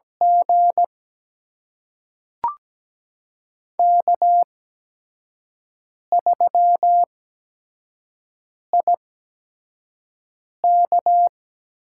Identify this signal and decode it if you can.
Morse code, heard as a keyed tone, about 17 wpm, spaced as for 5 wpm, 700 Hz. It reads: G K3IK
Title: USA Call Signs - 4 Characters - 8x Character Spacing - 17wpm